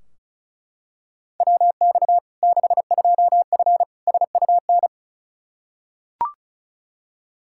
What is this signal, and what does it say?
Morse code, heard as a keyed tone, about 36 wpm, 700 Hz. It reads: WX 62F SUN
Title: Practice QSO - Exchange 2 - Info - 35wpm